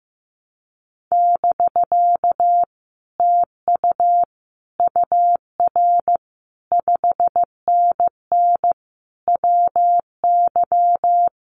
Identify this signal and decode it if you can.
Morse code, heard as a keyed tone, about 14 wpm, 700 Hz. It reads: <BK> TU UR 5NN WY
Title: Common QSO Phrases Encore 15wpm